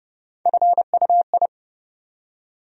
Morse code, heard as a keyed tone, about 30 wpm, 700 Hz. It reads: FUI
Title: USA Call Sign Postfixes - 3 Letters 30wpm